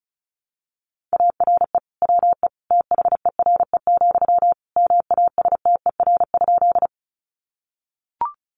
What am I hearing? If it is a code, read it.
Morse code, heard as a keyed tone, about 34 wpm, 700 Hz. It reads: ARE WE THERE, MASTER?
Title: Sentences from Top 1300 Words 35wpm